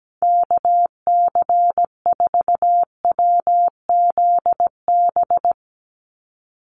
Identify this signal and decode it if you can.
Morse code, heard as a keyed tone, about 17 wpm, 700 Hz. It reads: KC4WZB
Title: NA SOTA Activator Callsigns 17wpm